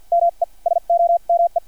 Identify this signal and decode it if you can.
Morse code, heard as a keyed tone, about 50 wpm, spaced as for 41 wpm, 690 Hz. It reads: ME SOME
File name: archived / CRSnet